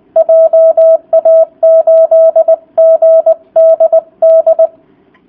Morse code, heard as a keyed tone, about 20 wpm, 650 Hz. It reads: JA8GDD